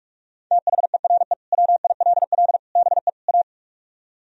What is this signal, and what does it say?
Morse code, heard as a keyed tone, about 44 wpm, 700 Hz. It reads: THERE WILL BE A